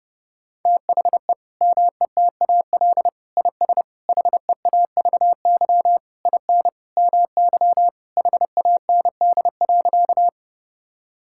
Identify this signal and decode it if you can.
Morse code, heard as a keyed tone, about 30 wpm, 700 Hz. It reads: THE METAL IS HEAVY IN MY HAND.
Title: Sentences from Top 1000 Words - Review Part 6 - 30wpm